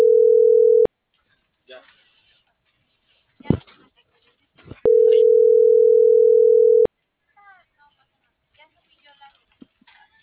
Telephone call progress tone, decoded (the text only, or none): ringback tone